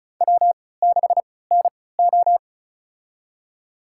Morse code, heard as a keyed tone, about 36 wpm, spaced as for 20 wpm, 690 Hz. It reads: W6NO
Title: USA Call Signs - 4 Characters - 3x Character Spacing - 35wpm